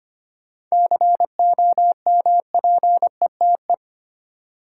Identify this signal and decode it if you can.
Morse code, heard as a keyed tone, about 25 wpm, 700 Hz. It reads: COMPETE
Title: Seven Letter Words - 25wpm